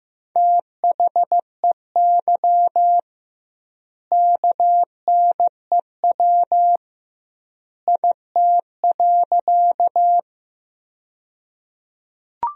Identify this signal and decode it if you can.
Morse code, heard as a keyed tone, about 15 wpm, 700 Hz. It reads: THEY KNEW IT.
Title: 2x Word-Spacing - Sentences from Top 400 Words - 15wpm